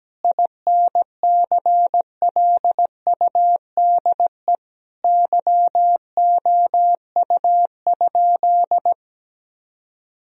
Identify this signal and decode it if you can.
Morse code, heard as a keyed tone, about 17 wpm, 700 Hz. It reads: INCLUDE YOU?
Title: Sentences from Top 1000 Words - Review Part 3 - 17wpm